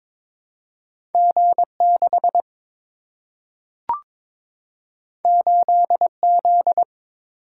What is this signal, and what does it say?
Morse code, heard as a keyed tone, about 22 wpm, 700 Hz. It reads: G 6 8 Z
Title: Single-Letter Single-Number 22wpm